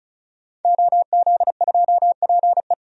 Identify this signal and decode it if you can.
Morse code, heard as a keyed tone, about 36 wpm, 700 Hz. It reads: OZ2PE
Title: DXCC-100 Call Signs - 5 Characters - 35wpm